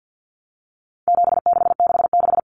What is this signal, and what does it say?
Morse code, heard as a keyed tone, about 50 wpm, 700 Hz. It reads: 7666
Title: Numbers - 4-Digit 50wpm